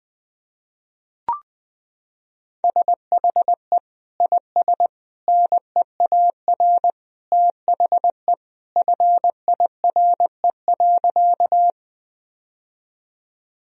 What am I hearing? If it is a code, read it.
Morse code, heard as a keyed tone, about 21 wpm, 700 Hz. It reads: SHE IS NEAR THE FIRE.
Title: Sentences from Top 1000 Words - Review Part 1 - 20wpm